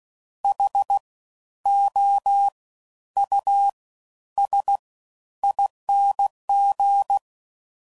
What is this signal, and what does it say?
Morse code, heard as a keyed tone, about 16 wpm, spaced as for 11 wpm, 800 Hz. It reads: HOUSING
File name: Suffixes / Special INGs / Fast Ings